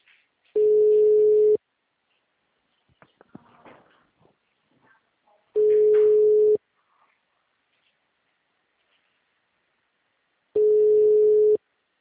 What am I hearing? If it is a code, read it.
Ringback tone (Europe, 425 Hz).